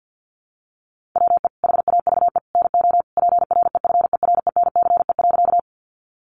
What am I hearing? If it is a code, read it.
Morse code, heard as a keyed tone, about 50 wpm, 700 Hz. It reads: WE HAVE NO PREFERENCE.
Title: Sentences from Top 1500 Words 50wpm